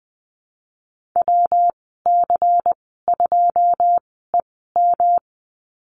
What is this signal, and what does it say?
Morse code, heard as a keyed tone, about 20 wpm, spaced as for 15 wpm, 700 Hz. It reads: WC2EM
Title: 161 Lesson 26g - Letter g - Call Signs 20wpm